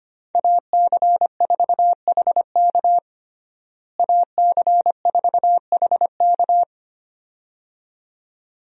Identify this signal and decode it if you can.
Morse code, heard as a keyed tone, about 25 wpm, 700 Hz. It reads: AC4HK AC4HK
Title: USA Call Signs - 5 Characters - Repeated 2x - 25wpm